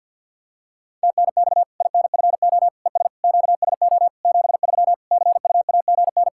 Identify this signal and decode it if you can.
Morse code, heard as a keyed tone, about 50 wpm, 700 Hz. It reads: TNX INFO ES QSO 73 CUAGN
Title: Practice QSO - Exchange 4 - Ending - 50wpm